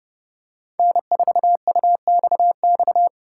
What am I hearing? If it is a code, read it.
Morse code, heard as a keyed tone, about 32 wpm, 700 Hz. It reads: N4UXX
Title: USA Call Signs - 5 Characters - 6x Character Spacing - 30wpm